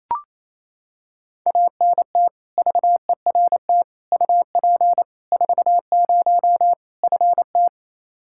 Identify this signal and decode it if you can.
Morse code, heard as a keyed tone, about 27 wpm, 700 Hz. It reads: ANT VERT UP 40 FT
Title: K1USN SST QSO Practice 28wpm